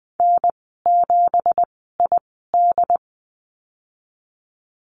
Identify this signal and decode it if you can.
Morse code, heard as a keyed tone, about 20 wpm, spaced as for 15 wpm, 700 Hz. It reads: N7ID
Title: 207 Lesson 33g - Letter j - Call Signs 20wpm